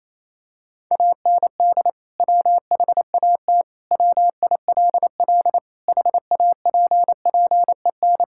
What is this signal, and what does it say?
Morse code, heard as a keyed tone, about 27 wpm, 700 Hz. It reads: AND WHAT WILL HAPPEN
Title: Sentences from Top 1000 Words - Review Part 5 - 28wpm